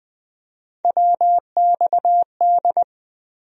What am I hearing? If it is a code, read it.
Morse code, heard as a keyed tone, about 20 wpm, 700 Hz. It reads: WXD
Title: USA Call Sign Postfixes - 3 Letters 20wpm